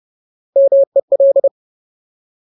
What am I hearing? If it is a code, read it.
Morse code, heard as a keyed tone, about 30 wpm, 550 Hz. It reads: MEL
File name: Ultimate Word List / 30WPM Ultimate